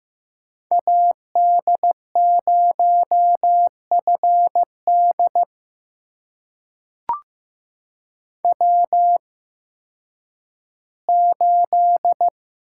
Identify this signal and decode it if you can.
Morse code, heard as a keyed tone, about 15 wpm, spaced as for 10 wpm, 700 Hz. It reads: AD0FD W8
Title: USA Call Signs - 5 Characters - 8x Character Spacing - 15wpm